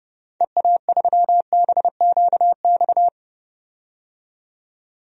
Morse code, heard as a keyed tone, about 30 wpm, 700 Hz. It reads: EA3BQX
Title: DXCC-100 Call Signs - 6 Characters - 30wpm